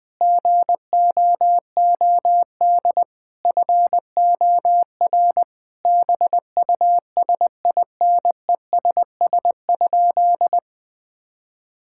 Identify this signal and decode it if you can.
Morse code, heard as a keyed tone, about 20 wpm, 700 Hz. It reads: GOOD FOR BUSINESS?
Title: Sentences from Top 1000 Words - Review Part 6 - 20wpm